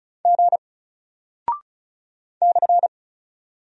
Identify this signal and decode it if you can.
Morse code, heard as a keyed tone, about 36 wpm, 700 Hz. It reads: G /
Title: Single Character (Letter:Number:Punctuation:Prosign) 35wpm